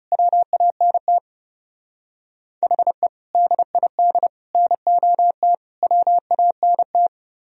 Morse code, heard as a keyed tone, about 37 wpm, 700 Hz. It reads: WANT HE DID NOT WANT
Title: Speed-Racing - Sets of 4 Words 30wpm